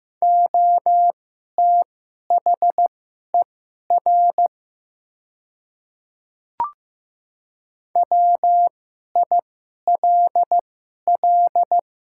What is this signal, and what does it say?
Morse code, heard as a keyed tone, about 15 wpm, spaced as for 11 wpm, 700 Hz. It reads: OTHER WILL
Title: 2x Character Spacing - Top 100 Words - 15wpm